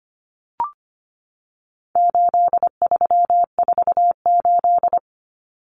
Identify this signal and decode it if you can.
Morse code, heard as a keyed tone, about 25 wpm, 700 Hz. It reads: 8348
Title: Common Cut Numbers - 4-Digit 25wpm